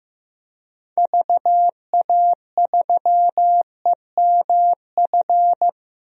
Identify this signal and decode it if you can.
Morse code, heard as a keyed tone, about 16 wpm, 700 Hz. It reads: VA3EMF
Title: DXCC-100 Call Signs - 6 Characters - 15wpm